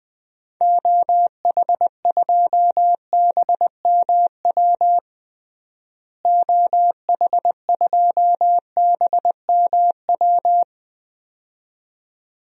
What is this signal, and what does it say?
Morse code, heard as a keyed tone, about 21 wpm, 700 Hz. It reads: OH2BMW OH2BMW
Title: DXCC-100 Call Signs - 6 Characters - Repeated 2x - 20wpm